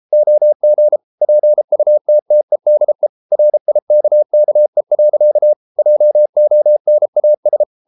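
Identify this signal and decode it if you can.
Morse code, heard as a keyed tone, about 33 wpm, 600 Hz. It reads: OG PUTTEDE RIKKE. JONAS
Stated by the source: Morse code at 33 wpm, QRQ.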